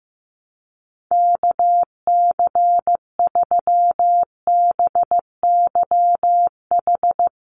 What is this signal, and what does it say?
Morse code, heard as a keyed tone, about 15 wpm, 700 Hz. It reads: KC3BYH
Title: USA Call Signs 15wpm